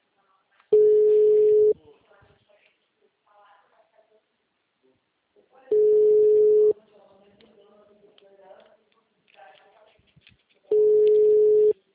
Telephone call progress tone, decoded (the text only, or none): ringback tone